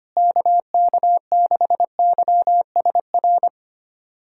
Morse code, heard as a keyed tone, about 25 wpm, 700 Hz. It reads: KK6YSR
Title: USA Callsign DE Callsign - 6 Characters - 25wpm